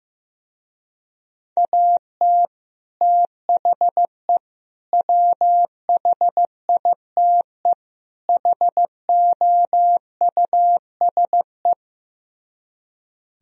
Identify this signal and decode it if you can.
Morse code, heard as a keyed tone, about 15 wpm, 700 Hz. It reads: AT THE WHITE HOUSE
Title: Speed-Racing - Sets of 4 Words 15wpm